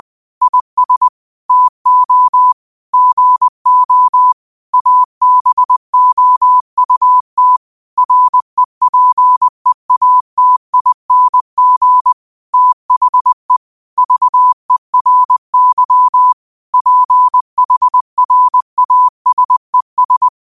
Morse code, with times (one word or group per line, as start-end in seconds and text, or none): 0.4-1.1 IS
1.5-2.5 TO
2.9-4.3 GO
4.7-7.6 ABOUT
8.0-12.1 REPEATING
12.5-13.6 THE
14.0-16.3 VERY
16.7-20.3 PHRASES